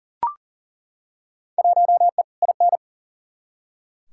Morse code, heard as a keyed tone, about 39 wpm, 700 Hz. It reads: 1E IN
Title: Field Day Exchanges - 40wpm